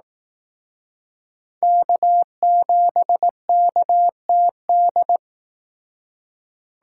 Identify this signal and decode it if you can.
Morse code, heard as a keyed tone, about 18 wpm, 700 Hz. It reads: K7KTD
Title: Speed Racing - USA 1x3 Call Signs 15wpm